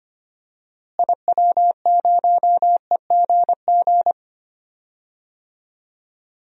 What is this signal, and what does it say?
Morse code, heard as a keyed tone, about 25 wpm, 700 Hz. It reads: IW0EGG
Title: DXCC-100 Call Signs - 6 Characters - Repeated 2x - 25wpm